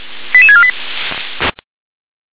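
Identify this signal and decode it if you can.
Five-tone selective call (ZVEI), heard as quick runs of five tones, tones of about 80 ms. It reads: 88648
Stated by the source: ZVEI five-tone selective call received on an SDR.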